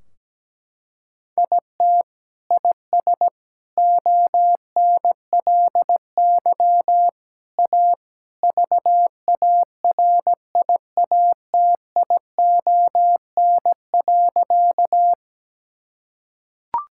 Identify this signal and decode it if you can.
Morse code, heard as a keyed tone, about 17 wpm, 700 Hz. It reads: IT IS ONLY A VARIATION.
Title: Sentences from Top 1600 Words 17wpm